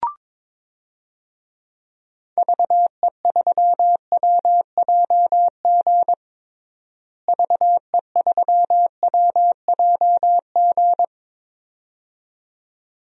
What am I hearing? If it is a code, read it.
Morse code, heard as a keyed tone, about 21 wpm, 700 Hz. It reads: VE3WJG VE3WJG
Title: DXCC-100 Call Signs - 6 Characters - Repeated 2x - 22wpm